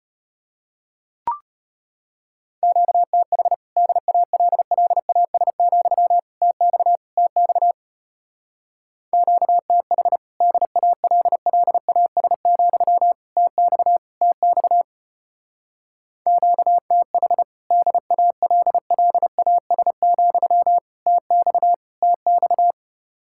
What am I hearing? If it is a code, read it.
Morse code, heard as a keyed tone, about 32 wpm, 700 Hz. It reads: QTH DALLAS, TX TX QTH DALLAS, TX TX QTH DALLAS, TX TX